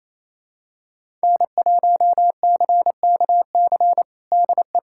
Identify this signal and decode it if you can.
Morse code, heard as a keyed tone, about 28 wpm, 700 Hz. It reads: N1CKC DE
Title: USA Callsign DE Callsign - 5 Characters - 28wpm